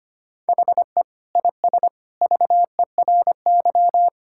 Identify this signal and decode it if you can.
Morse code, heard as a keyed tone, about 25 wpm, 700 Hz. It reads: HE IS VERY
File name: practice / 25wpm / Sentences with 2-Word Phrases - 25wpm